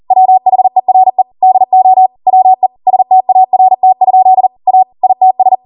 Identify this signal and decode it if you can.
Morse code, heard as a keyed tone, about 40 wpm, 750 Hz. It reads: WHERE DO WE START? A ITS
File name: Morse / 40WPM